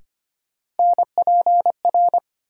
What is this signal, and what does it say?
Morse code, heard as a keyed tone, about 25 wpm, 700 Hz. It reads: NPR
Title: Sets of 5 Words - New Word Drill - 25wpm